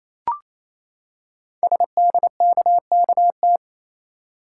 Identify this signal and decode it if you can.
Morse code, heard as a keyed tone, about 28 wpm, 700 Hz. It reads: SDKKT
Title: Code Groups - 5 Letters - 28wpm